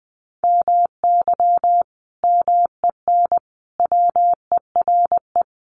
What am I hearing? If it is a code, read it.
Morse code, heard as a keyed tone, about 20 wpm, 700 Hz. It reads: MY MEN WERE
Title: Sentences from Top 1100 Words 20wpm